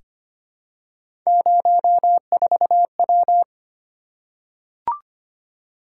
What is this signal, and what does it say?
Morse code, heard as a keyed tone, about 25 wpm, 700 Hz. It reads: 04W
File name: Three Letter-Number Combinations - 25wpm